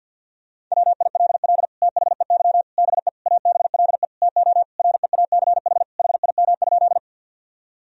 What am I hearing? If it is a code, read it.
Morse code, heard as a keyed tone, about 52 wpm, 700 Hz. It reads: WILL THEY BE ABLE TO REACH HIM?
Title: Sentences from Top 1000 Words - Review Part 2 - 50wpm